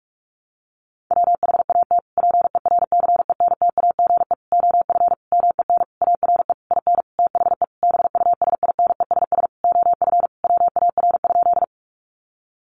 Morse code, heard as a keyed tone, about 44 wpm, 700 Hz. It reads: WHAT PERCENTAGE OF MEN ARE IN THE BUSINESS OF WAR?
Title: Sentences from Top 1300 Words 45wpm